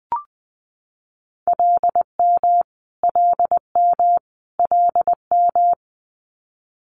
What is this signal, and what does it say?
Morse code, heard as a keyed tone, about 20 wpm, 700 Hz. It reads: LM LM LM
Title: Multi-letter Phonemes Repeated 3x 20wpm